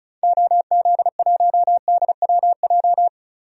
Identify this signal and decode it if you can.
Morse code, heard as a keyed tone, about 36 wpm, 700 Hz. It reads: OZ1DWJ
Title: DXCC-100 Call Signs - 6 Characters - 35wpm